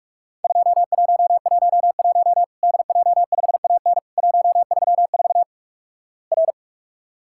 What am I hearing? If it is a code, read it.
Morse code, heard as a keyed tone, about 48 wpm, 700 Hz. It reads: 2111 DJ5AN 134 R
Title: Worked All Europe DX Contest - QTC Exchange - 45wpm